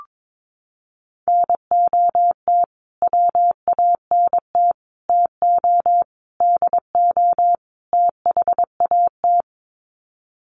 Morse code, heard as a keyed tone, about 22 wpm, 700 Hz. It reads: NOT WANT TO DO THAT
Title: Sets of 5 Words Encore 22wpm